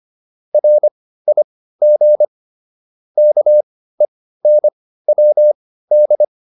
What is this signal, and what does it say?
Morse code, heard as a keyed tone, about 25 wpm, spaced as for 15 wpm, 600 Hz. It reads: RIG KENWD
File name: QSO308 15